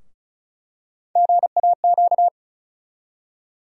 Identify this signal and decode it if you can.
Morse code, heard as a keyed tone, about 34 wpm, 700 Hz. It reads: GAQ
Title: USA Call Sign Postfixes - 3 Letters 35wpm